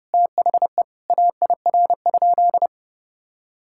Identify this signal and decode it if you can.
Morse code, heard as a keyed tone, about 32 wpm, 700 Hz. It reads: THE AIR?
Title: Sentences from Top 1000 Words - Review Part 6 - 30wpm